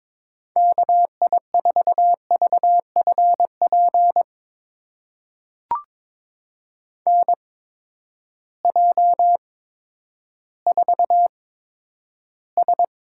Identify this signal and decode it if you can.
Morse code, heard as a keyed tone, about 22 wpm, spaced as for 11 wpm, 700 Hz. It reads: KI4VFP NJ4S
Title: USA Call Signs - 8x Character Spacing - 22wpm